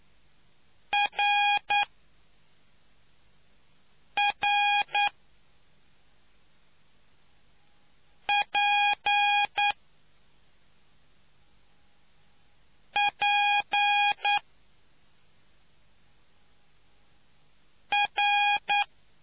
Morse code, heard as a keyed tone, about 9 wpm, 820 Hz. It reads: R R P P R